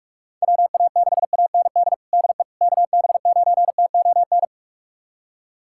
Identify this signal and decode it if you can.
Morse code, heard as a keyed tone, about 46 wpm, 700 Hz. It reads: WA6AND DE KB9TON
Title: USA Callsign DE Callsign 45wpm